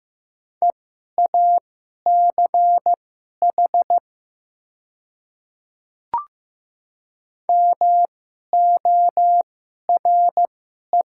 Morse code, heard as a keyed tone, about 15 wpm, spaced as for 11 wpm, 700 Hz. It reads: EACH MORE
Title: 2x Character Spacing - Top 200 Words - 15wpm